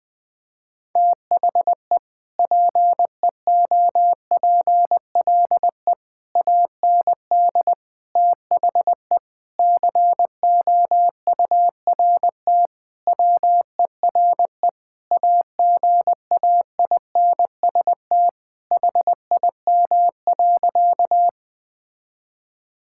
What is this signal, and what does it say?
Morse code, heard as a keyed tone, about 20 wpm, 700 Hz. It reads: THE PEOPLE AND THE COURT WERE AGAINST HIM.
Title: Sentences from Top 1600 Words 20wpm